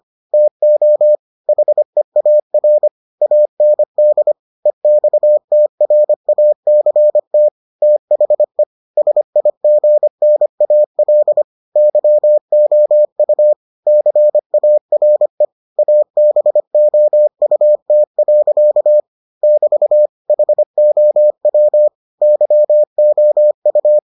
Morse code, heard as a keyed tone, about 25 wpm, 600 Hz. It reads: TO HEAR AND EXTRACT THE SIGNAL YOU CARE ABOUT. <BT> HOW YOU